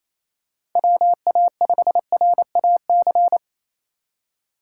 Morse code, heard as a keyed tone, about 28 wpm, 700 Hz. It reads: WA5RAC
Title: USA Call Signs - 6 Characters - 2x Character Spacing - 28wpm